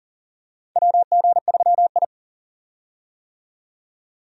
Morse code, heard as a keyed tone, about 40 wpm, 700 Hz. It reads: WG3I
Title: USA Call Signs - 4 Characters - 40wpm